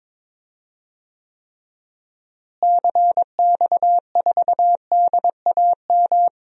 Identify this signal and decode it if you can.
Morse code, heard as a keyed tone, about 22 wpm, 700 Hz. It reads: CX4DAM
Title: DXCC-100 Call Signs - 6 Characters - Repeated 2x - 22wpm